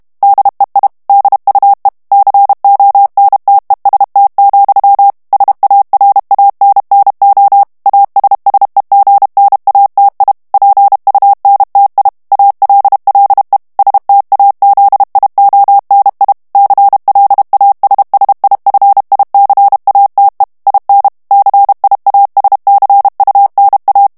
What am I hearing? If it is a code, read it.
Morse code, heard as a keyed tone, about 32 wpm, 800 Hz. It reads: DEI DUE CONTEST, SARANNO ASSEGNATI PUNTI ALLE STAZIONI CLASSIFICATE IN CIASCUNA
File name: ITALIEN texte clair 2 vitesse 30 mots minute